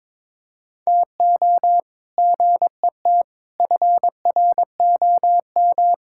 Morse code, heard as a keyed tone, about 21 wpm, 700 Hz. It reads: TO GET FROM
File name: practice / 22wpm / Sets of 8 Words - 22wpm